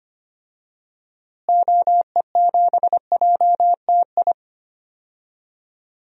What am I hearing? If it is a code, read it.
Morse code, heard as a keyed tone, about 25 wpm, 700 Hz. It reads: OE7JTI
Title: DXCC-100 Call Signs - 6 Characters - Repeated 3x - 25wpm